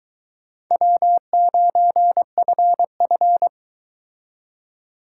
Morse code, heard as a keyed tone, about 22 wpm, 700 Hz. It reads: W9FF